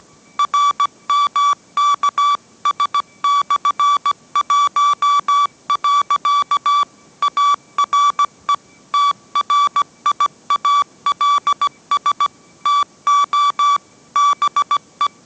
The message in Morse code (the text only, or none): RMKS/1. ARE TRIALS TO BE